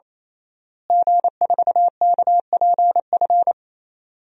Morse code, heard as a keyed tone, about 28 wpm, 700 Hz. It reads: G4KPF
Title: DXCC-100 Call Signs - 5 Characters - Repeated 3x - 28wpm